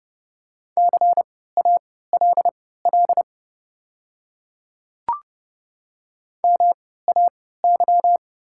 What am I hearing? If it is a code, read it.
Morse code, heard as a keyed tone, about 30 wpm, spaced as for 17 wpm, 690 Hz. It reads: CALL MAY
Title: 3x Character Spacing - Top 300 Words - 30wpm